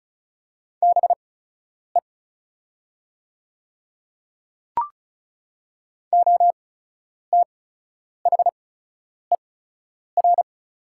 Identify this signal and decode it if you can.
Morse code, heard as a keyed tone, about 36 wpm, spaced as for 10 wpm, 690 Hz. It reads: BE OTHER
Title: 8x Character Spacing - Top 100 Words - 35wpm